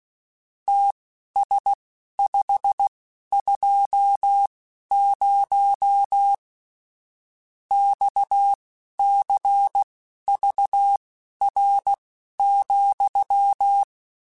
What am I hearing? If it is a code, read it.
Morse code, heard as a keyed tone, about 16 wpm, spaced as for 11 wpm, 800 Hz. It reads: TS520 XCVR,
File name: (spaced) / QSOs Spaced-mp3s / WD0FBT 1 @15 sp